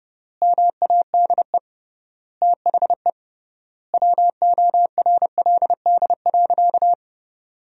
Morse code, heard as a keyed tone, about 30 wpm, 700 Hz. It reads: MADE THE WORLD.
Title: 3x Word-Spacing - Sentences from Top 200 Words - 30wpm